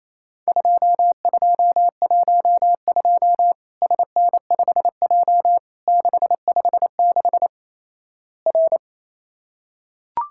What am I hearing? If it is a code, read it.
Morse code, heard as a keyed tone, about 28 wpm, 700 Hz. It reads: 2212 SN5J 656 R
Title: Worked All Europe DX Contest - QTC Exchange - 28wpm